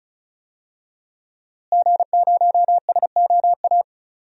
Morse code, heard as a keyed tone, about 36 wpm, 700 Hz. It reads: G0SOA
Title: DXCC-100 Call Signs - 5 Characters - Repeated 3x - 35wpm